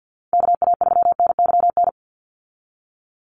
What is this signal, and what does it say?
Morse code, heard as a keyed tone, about 50 wpm, 700 Hz. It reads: KA3NYN
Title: USA Call Signs 50wpm